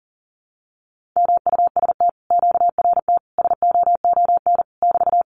Morse code, heard as a keyed tone, about 40 wpm, 700 Hz. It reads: MUST QRT SOON <BT>
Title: Common QSO Phrases Encore 40wpm